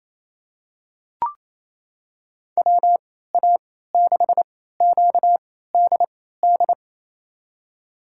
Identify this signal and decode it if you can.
Morse code, heard as a keyed tone, about 28 wpm, spaced as for 16 wpm, 700 Hz. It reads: WA6QDD
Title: USA Call Signs - 6 Characters - 3x Character Spacing - 28wpm